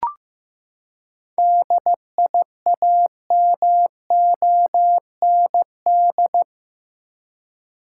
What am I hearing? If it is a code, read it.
Morse code, heard as a keyed tone, about 15 wpm, 700 Hz. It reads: DIAMOND
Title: Seven Letter Words - 15wpm